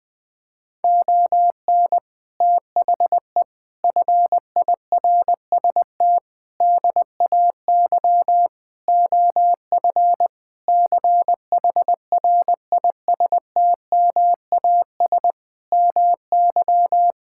Morse code, heard as a keyed tone, about 20 wpm, 700 Hz. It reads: ON THE FIRST DAY OF CHRISTMAS MY
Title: Sentences with 8-Word Phrases - Spoken Beginning - 20wpm